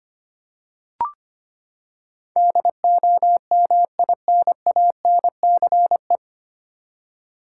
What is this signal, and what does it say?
Morse code, heard as a keyed tone, about 25 wpm, 700 Hz. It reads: DOMINANCE
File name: Nine Letter Words - 25wpm